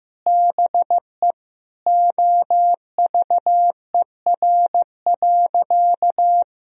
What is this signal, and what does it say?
Morse code, heard as a keyed tone, about 16 wpm, 700 Hz. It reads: BE OVER.
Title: Sentences from Top 1000 Words - Review Part 2 - 15wpm